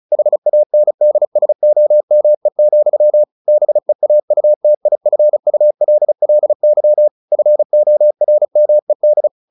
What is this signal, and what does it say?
Morse code, heard as a keyed tone, about 35 wpm, 600 Hz. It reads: HANDSOME, BEAUTIFULLY FORMED
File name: the old man and the sea 2 35wpm